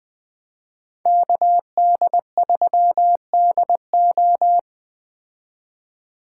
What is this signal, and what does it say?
Morse code, heard as a keyed tone, about 20 wpm, 700 Hz. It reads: KD3DO
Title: Speed Racing - USA Call Signs 17wpm